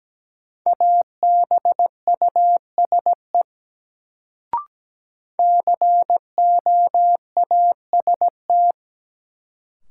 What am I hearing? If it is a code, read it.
Morse code, heard as a keyed tone, about 17 wpm, 700 Hz. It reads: ABUSE COAST
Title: Sentences from Top 1300 Words - New Word Drill Alt - 17wpm